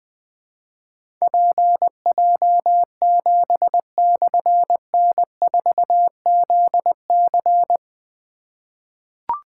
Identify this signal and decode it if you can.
Morse code, heard as a keyed tone, about 20 wpm, 700 Hz. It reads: PJ7/N4ZC
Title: Difficult Call Signs 20wpm